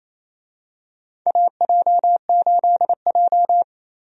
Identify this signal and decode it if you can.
Morse code, heard as a keyed tone, about 27 wpm, 700 Hz. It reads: AJ8J
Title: USA Call Signs - 4 Characters - 28wpm